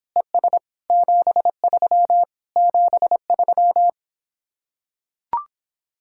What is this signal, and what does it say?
Morse code, heard as a keyed tone, about 26 wpm, 700 Hz. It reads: ES 73 73